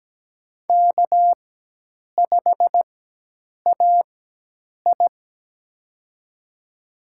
Morse code, heard as a keyed tone, about 17 wpm, spaced as for 8 wpm, 700 Hz. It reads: K5AI
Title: USA Call Signs - 4 Characters - 4x Character Spacing - 17wpm